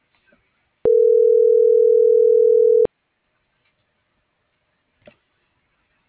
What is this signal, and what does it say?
Ringback tone (North America), 440 + 480 Hz.